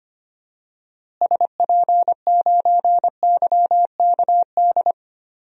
Morse code, heard as a keyed tone, about 25 wpm, 700 Hz. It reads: SP9YKD
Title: DXCC-100 Call Signs - 6 Characters - Repeated 2x - 25wpm